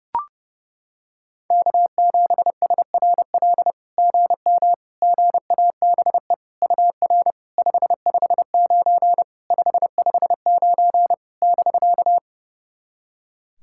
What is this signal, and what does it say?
Morse code, heard as a keyed tone, about 28 wpm, 690 Hz. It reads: K7SRL GM GABE UR 559 559 <BK>
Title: SOTA QSO Practice 30wpm